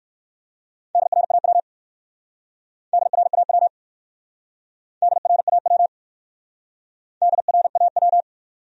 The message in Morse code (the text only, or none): DRAW DRAW DRAW DRAW